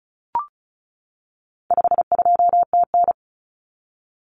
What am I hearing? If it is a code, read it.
Morse code, heard as a keyed tone, about 34 wpm, 700 Hz. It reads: 52TN
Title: Common Cut Numbers - 4-Digit 35wpm